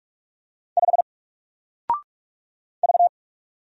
Morse code, heard as a keyed tone, about 45 wpm, 700 Hz. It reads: 5 V
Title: Single Letter-Number Alt 45wpm